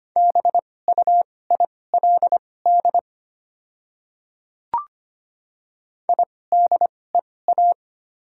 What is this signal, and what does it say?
Morse code, heard as a keyed tone, about 25 wpm, spaced as for 18 wpm, 690 Hz. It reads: BUILD IDEA
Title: 2x Character Spacing - Top 300 Words - 25wpm